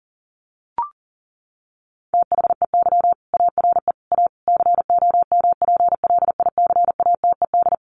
Morse code, heard as a keyed tone, about 40 wpm, 700 Hz. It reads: THEY ARE A COMPLICATED
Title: Sentences from Top 1200 Words 40wpm